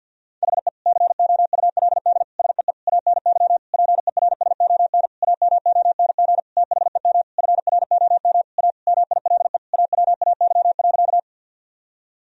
Morse code, heard as a keyed tone, about 49 wpm, 700 Hz. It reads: HE COULD SEE ANY PERSON AMONG THEM FROM A MILE AWAY.
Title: Sentences from Top 1000 Words - Review Part 3 - 50wpm